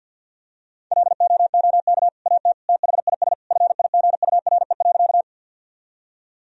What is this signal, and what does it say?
Morse code, heard as a keyed tone, about 50 wpm, 700 Hz. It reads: LOOK AT THIS FIGURE.
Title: Sentences from Top 1000 Words - Review Part 2 - 50wpm